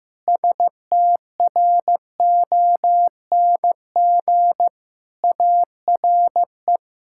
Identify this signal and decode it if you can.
Morse code, heard as a keyed tone, about 16 wpm, 700 Hz. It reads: STRONG ARE
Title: Sentences from Top 1000 Words - Review Part 5 - 15wpm